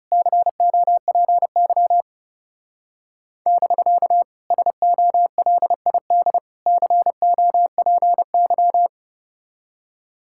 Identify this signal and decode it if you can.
Morse code, heard as a keyed tone, about 35 wpm, 700 Hz. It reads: COPY <BK> SOLID COPY
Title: Speed-Racing Common QSO Phrases - 30wpm